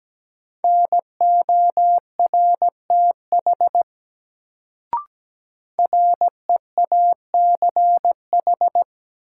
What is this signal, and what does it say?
Morse code, heard as a keyed tone, about 17 wpm, 700 Hz. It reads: NORTH REACH
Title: Sentences from Top 400 Words - New Word Drill - 17wpm